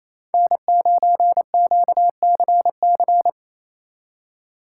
Morse code, heard as a keyed tone, about 29 wpm, 700 Hz. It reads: N9QCC
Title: Speed Racing - USA 1x3 Call Signs 28wpm